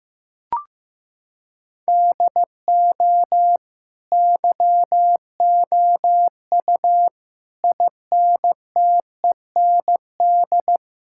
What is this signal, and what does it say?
Morse code, heard as a keyed tone, about 16 wpm, 700 Hz. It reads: DO YOU INTEND
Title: Sentences from Top 1600 Words 15wpm